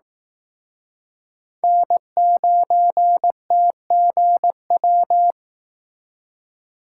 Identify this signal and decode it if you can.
Morse code, heard as a keyed tone, about 18 wpm, 700 Hz. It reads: N9TGW
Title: Speed Racing - USA 1x3 Call Signs 15wpm